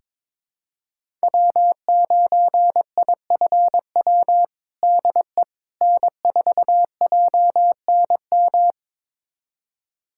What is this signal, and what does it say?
Morse code, heard as a keyed tone, about 22 wpm, 700 Hz. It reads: W9IFW DE N4JNM
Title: USA Callsign DE Callsign - 5 Characters - 22wpm